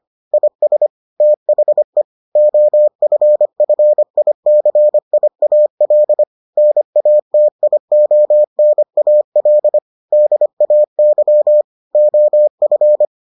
Morse code, heard as a keyed tone, about 25 wpm, 600 Hz. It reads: IS THE OFFICIAL NATIONAL DAY OF